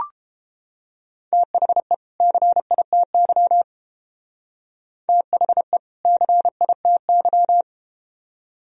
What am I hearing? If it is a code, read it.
Morse code, heard as a keyed tone, about 31 wpm, 700 Hz. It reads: THE CITY THE CITY